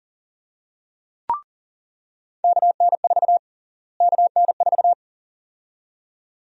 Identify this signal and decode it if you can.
Morse code, heard as a keyed tone, about 40 wpm, 700 Hz. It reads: KN4 KN4
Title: USA Call Sign Prefixes - Repeated 2x - 40wpm